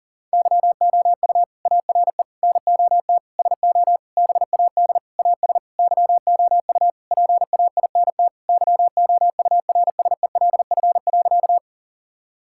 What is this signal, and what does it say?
Morse code, heard as a keyed tone, about 40 wpm, 700 Hz. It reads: YOU ARE NOT SO BAD AS YOU PAINT YOURSELF.
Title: Sentences from Top 1000 Words - Review Part 6 - 40wpm